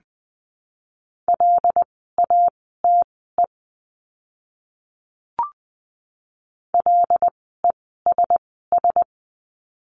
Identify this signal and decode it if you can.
Morse code, heard as a keyed tone, about 20 wpm, spaced as for 15 wpm, 700 Hz. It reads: LATE LESS
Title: 094 Lesson 16f - Letter c - Words 20wpm